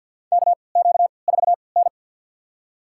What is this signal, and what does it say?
Morse code, heard as a keyed tone, about 52 wpm, spaced as for 29 wpm, 690 Hz. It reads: KQ4N
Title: USA Call Signs - 3x Character Spacing - 50wpm